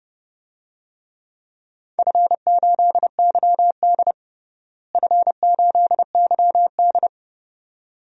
Morse code, heard as a keyed tone, about 32 wpm, 700 Hz. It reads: F8YD F8YD
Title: DXCC-100 Call Signs - 4 Characters - Repeated 2x - 30wpm